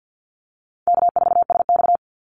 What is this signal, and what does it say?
Morse code, heard as a keyed tone, about 50 wpm, 700 Hz. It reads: K4SX
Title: Call Signs Repeated 2x - 50wpm